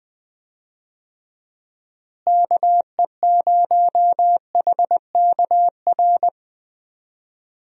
Morse code, heard as a keyed tone, about 21 wpm, 700 Hz. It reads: KE0HKR
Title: USA Call Signs - 6 Characters - Repeated 3x - 20wpm